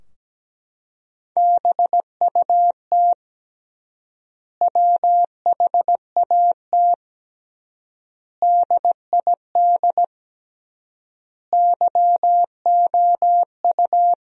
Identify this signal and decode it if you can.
Morse code, heard as a keyed tone, about 17 wpm, 700 Hz. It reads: BUT WHAT DID YOU
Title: 3x Word-Spacing - Sentences from Top 100 Words - 17wpm